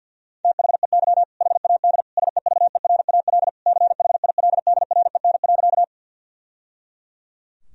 Morse code, heard as a keyed tone, about 52 wpm, 700 Hz. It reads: THEY HAD SEVERAL CHILDREN.
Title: Sentences from Top 1000 Words - Review Part 2 - 50wpm